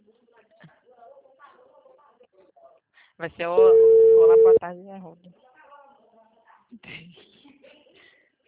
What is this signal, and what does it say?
Ringback tone (China, 450 Hz), with speech.